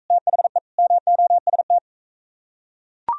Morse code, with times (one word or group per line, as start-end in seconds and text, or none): 0.1-0.6 THE
0.8-1.8 MOST